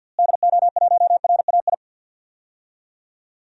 Morse code, heard as a keyed tone, about 52 wpm, 700 Hz. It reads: DO1RAI
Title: DXCC-100 Call Signs - 6 Characters - Repeated 2x - 50wpm